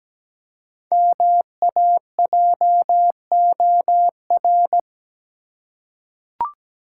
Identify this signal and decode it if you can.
Morse code, heard as a keyed tone, about 17 wpm, 700 Hz. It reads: MAJOR
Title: Five Letter Words - 17wpm